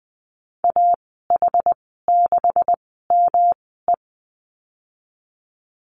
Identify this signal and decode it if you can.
Morse code, heard as a keyed tone, about 20 wpm, spaced as for 14 wpm, 700 Hz. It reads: AH6ME
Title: 128 Lesson 21g - Number 6 - Call Signs 20wpm